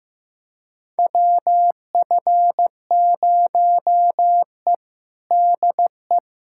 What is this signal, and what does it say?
Morse code, heard as a keyed tone, about 15 wpm, 700 Hz. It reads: WF0E DE
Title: USA Callsign DE Callsign - 4 Characters - 15wpm